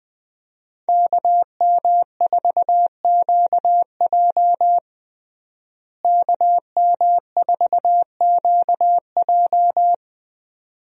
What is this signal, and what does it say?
Morse code, heard as a keyed tone, about 20 wpm, 700 Hz. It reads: KM4QJ KM4QJ
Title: USA Call Signs - 5 Characters - Repeated 2x - 20wpm